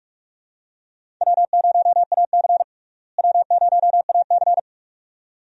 Morse code, heard as a keyed tone, about 47 wpm, 700 Hz. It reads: W0AC W0AC
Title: USA Call Signs - 4 Characters - Repeated 2x - 45wpm